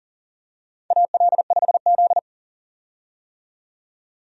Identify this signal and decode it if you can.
Morse code, heard as a keyed tone, about 40 wpm, 700 Hz. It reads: AL5Z
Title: USA Call Signs - 4 Characters - 40wpm